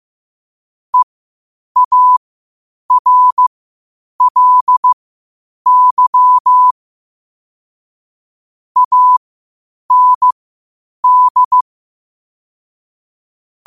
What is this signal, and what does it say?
Morse code, heard as a keyed tone, about 15 wpm, spaced as for 8 wpm, 1000 Hz. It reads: EARLY AND
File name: qotdfiles / 8wpm